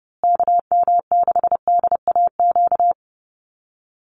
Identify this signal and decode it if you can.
Morse code, heard as a keyed tone, about 30 wpm, 700 Hz. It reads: KM6DAQ
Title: USA Call Signs Repeated 3x 30wpm